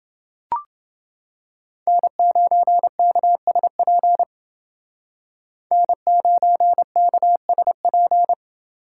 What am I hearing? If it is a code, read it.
Morse code, heard as a keyed tone, about 28 wpm, 700 Hz. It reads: N9KSP N9KSP